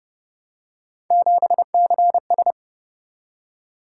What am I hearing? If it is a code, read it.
Morse code, heard as a keyed tone, about 31 wpm, 700 Hz. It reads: 7CS
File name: Speed-Racing - Three Letter-Number Combinations - 25wpm